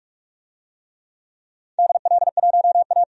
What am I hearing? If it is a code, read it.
Morse code, heard as a keyed tone, about 45 wpm, 700 Hz. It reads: DL1A
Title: DXCC-100 Call Signs - 4 Characters - 45wpm